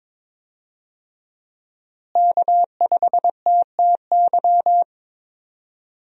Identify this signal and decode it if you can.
Morse code, heard as a keyed tone, about 22 wpm, 700 Hz. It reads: K5TTY
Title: USA Call Signs - 5 Characters - Repeated 3x - 22wpm